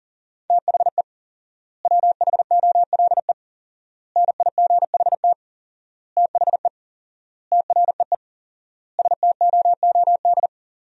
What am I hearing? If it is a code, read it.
Morse code, heard as a keyed tone, about 40 wpm, 700 Hz. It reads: THE WHOLE NIGHT THE TREE STOOD
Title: 4x Word-Spacing - Sentences from Top 500 Words - 40wpm